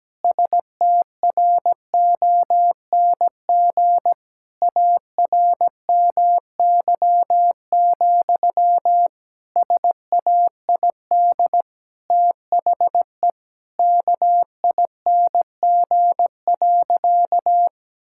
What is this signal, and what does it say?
Morse code, heard as a keyed tone, about 17 wpm, 700 Hz. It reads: STRONG ARMY, SAID THE KING.
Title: Sentences from Top 1000 Words - Review Part 5 - 17wpm